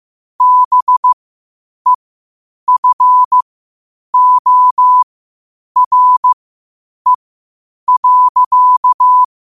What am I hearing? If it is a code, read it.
Morse code, heard as a keyed tone, about 15 wpm, spaced as for 8 wpm, 1000 Hz. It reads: BEFORE.
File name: qotdfiles / 8wpm